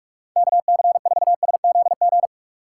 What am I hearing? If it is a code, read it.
Morse code, heard as a keyed tone, about 47 wpm, 700 Hz. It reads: KC4SZG
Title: USA Call Signs - 6 Characters - 4x Character Spacing - 45wpm